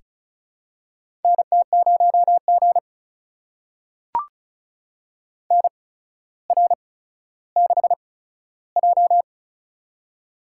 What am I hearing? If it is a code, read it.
Morse code, heard as a keyed tone, about 36 wpm, spaced as for 15 wpm, 700 Hz. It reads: NT0G NR6J
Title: USA Call Signs - 4 Characters - 8x Character Spacing - 35wpm